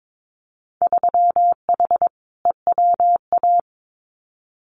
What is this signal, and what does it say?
Morse code, heard as a keyed tone, about 22 wpm, 700 Hz. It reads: 3H EWA
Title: Winter Field Day Exchanges Encore 22wpm